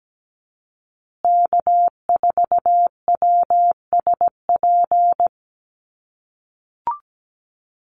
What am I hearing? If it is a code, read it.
Morse code, heard as a keyed tone, about 17 wpm, 700 Hz. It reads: K4WSP
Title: USA Call Signs Repeated 2x 17wpm